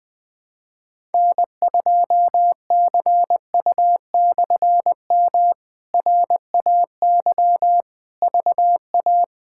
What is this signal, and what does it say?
Morse code, heard as a keyed tone, about 20 wpm, 700 Hz. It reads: N2CU/M RAY VA
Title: K1USN SST QSO Practice - 20wpm